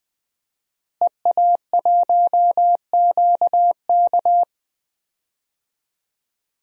Morse code, heard as a keyed tone, about 20 wpm, 700 Hz. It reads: EA1QK
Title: DXCC-100 Call Signs - 5 Characters - Repeated 3x - 20wpm